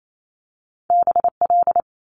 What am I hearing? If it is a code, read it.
Morse code, heard as a keyed tone, about 28 wpm, 700 Hz. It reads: BL
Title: Multi-letter Phonemes Quick Fire 28wpm